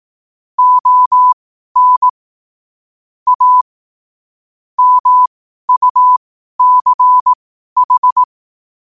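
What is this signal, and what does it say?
Morse code, heard as a keyed tone, about 18 wpm, spaced as for 12 wpm, 1000 Hz. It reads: ON A MUCH
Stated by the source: Morse sent at 12 wpm.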